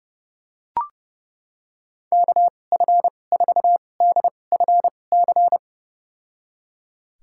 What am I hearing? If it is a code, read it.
Morse code, heard as a keyed tone, about 30 wpm, spaced as for 22 wpm, 700 Hz. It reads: KF4DFC
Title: USA Call Signs - 6 Characters - 2x Character Spacing - 30wpm